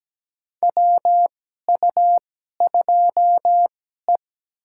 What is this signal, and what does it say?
Morse code, heard as a keyed tone, about 17 wpm, spaced as for 12 wpm, 700 Hz. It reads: WU2E
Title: USA Call Signs - 4 Characters - 2x Character Spacing - 17wpm